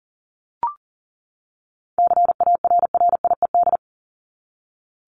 Morse code, heard as a keyed tone, about 38 wpm, 700 Hz. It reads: CARRIED
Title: ED Suffix - Top 100 Words with Base Words 40wpm